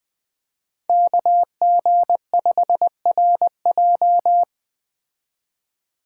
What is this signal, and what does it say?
Morse code, heard as a keyed tone, about 21 wpm, 700 Hz. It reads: KG5RJ